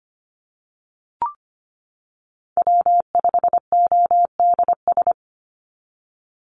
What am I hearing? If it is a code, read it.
Morse code, heard as a keyed tone, about 25 wpm, 700 Hz. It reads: W5ODS
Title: NA SOTA Chaser Callsigns 25wpm